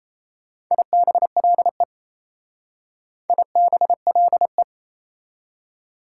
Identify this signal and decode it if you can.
Morse code, heard as a keyed tone, about 33 wpm, 700 Hz. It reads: IBLE IBLE
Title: Speed-Racing Prefixes and Suffixes 28wpm